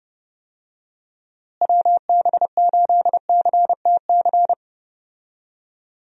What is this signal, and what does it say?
Morse code, heard as a keyed tone, about 30 wpm, 700 Hz. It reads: WB8CTC
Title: USA Call Signs - 6 Characters - Repeated 3x - 30wpm